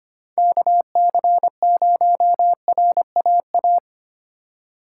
Morse code, heard as a keyed tone, about 33 wpm, 700 Hz. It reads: KC0RAA